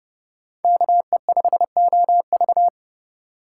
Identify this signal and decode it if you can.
Morse code, heard as a keyed tone, about 30 wpm, 700 Hz. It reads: KE5OV